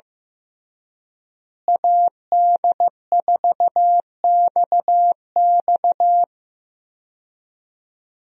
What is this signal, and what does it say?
Morse code, heard as a keyed tone, about 16 wpm, 700 Hz. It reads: AD4XX
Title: USA Call Signs - 5 Characters - Repeated 3x - 15wpm